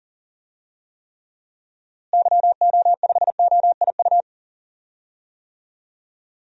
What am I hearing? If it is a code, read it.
Morse code, heard as a keyed tone, about 40 wpm, 700 Hz. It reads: YO5OIU
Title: DXCC-100 Call Signs - 6 Characters - Repeated 3x - 40wpm